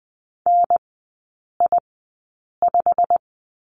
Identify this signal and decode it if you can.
Morse code, heard as a keyed tone, about 20 wpm, spaced as for 8 wpm, 700 Hz. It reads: NI5
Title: 107 Lesson 18e - Letter m - Three Characters 20wpm